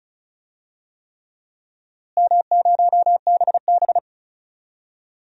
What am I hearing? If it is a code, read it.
Morse code, heard as a keyed tone, about 34 wpm, 700 Hz. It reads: M0BB
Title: DXCC-100 Call Signs - 4 Characters - Repeated 3x - 35wpm